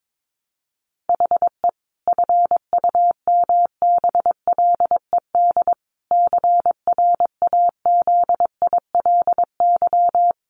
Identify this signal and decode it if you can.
Morse code, heard as a keyed tone, about 22 wpm, 700 Hz. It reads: HE FUMBLED CRAZILY
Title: A Sound Of Thunder 22wpm